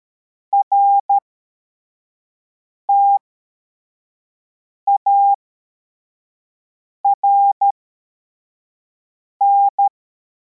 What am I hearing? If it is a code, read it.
Morse code, heard as a keyed tone, about 13 wpm, spaced as for 4 wpm, 800 Hz. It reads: RTARN
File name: formation / cw / Text11